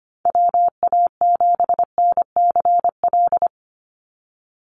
Morse code, heard as a keyed tone, about 25 wpm, 690 Hz. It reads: WA7NCL
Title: NA SOTA Activator Callsigns 25wpm